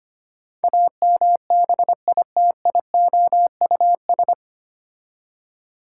Morse code, heard as a keyed tone, about 25 wpm, 700 Hz. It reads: AMBITIOUS
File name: practice / 25wpm / 9 Letter Words - First 3 Letters Given - 25wpm